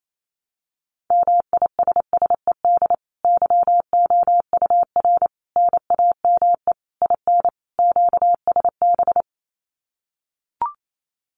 Morse code, heard as a keyed tone, about 28 wpm, 700 Hz. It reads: MISSED YOUR NAME IN QSB
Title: Common QSO Phrases 28wpm